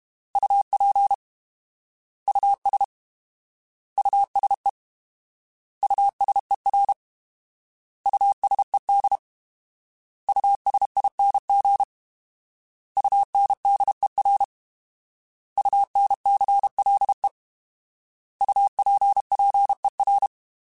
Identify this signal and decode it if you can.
Morse code, heard as a keyed tone, about 32 wpm, 800 Hz. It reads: UP US USE USER USED USING UNDER UNCLE UPPER
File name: Alphabetical / Alphabetical mp3s / Us @30